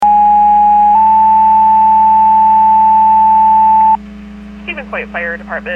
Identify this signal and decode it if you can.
Two-tone page, heard as a one-second tone, then a three-second tone, about 830 then 880 Hz, with speech. It is heard from 0.0 to 4.0 s.